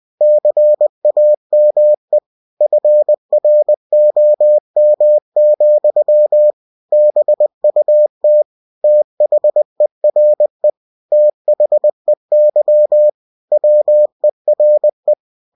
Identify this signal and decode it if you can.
Morse code, heard as a keyed tone, about 20 wpm, 600 Hz. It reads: CAME FROM, BUT THERE THEY WERE